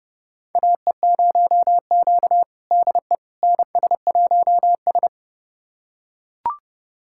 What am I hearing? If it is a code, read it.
Morse code, heard as a keyed tone, about 30 wpm, 700 Hz. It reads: AE0Q DE NS1S
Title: USA Callsign DE Callsign - 4 Characters - 30wpm